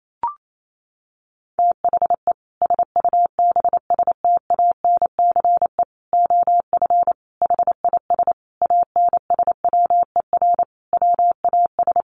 Morse code, heard as a keyed tone, about 28 wpm, 700 Hz. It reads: THE SUBSTANCE OF HIS ANSWER WAS
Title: Sentences from Top 1500 Words 28wpm